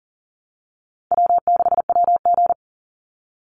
Morse code, heard as a keyed tone, about 40 wpm, 700 Hz. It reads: W6WG
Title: Call Signs Repeated 3x - 40wpm